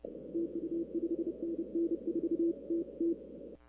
Morse code, heard as a keyed tone, about 30 wpm, 370 Hz. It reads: TU 5NN 4TT
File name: arrl cw dx